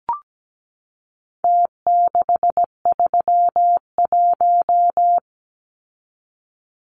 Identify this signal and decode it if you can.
Morse code, heard as a keyed tone, about 17 wpm, 700 Hz. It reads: T631
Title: Numbers - Contest Style - 4-Digit 17wpm